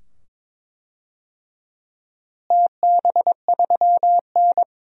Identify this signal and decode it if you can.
Morse code, heard as a keyed tone, about 22 wpm, 700 Hz. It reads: TB3N
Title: DXCC-100 Call Signs - 4 Characters - Repeated 2x - 22wpm